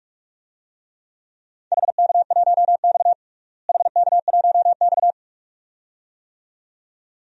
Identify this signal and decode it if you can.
Morse code, heard as a keyed tone, about 46 wpm, 700 Hz. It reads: HK1X HK1X
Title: DXCC-100 Call Signs - 4 Characters - Repeated 2x - 45wpm